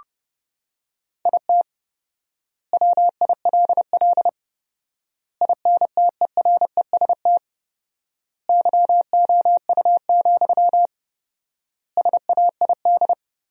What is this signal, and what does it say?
Morse code, heard as a keyed tone, about 30 wpm, 700 Hz. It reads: IT WILL INTEREST YOU, SAID
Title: 4x Word-Spacing - Sentences from Top 400 Words - 30wpm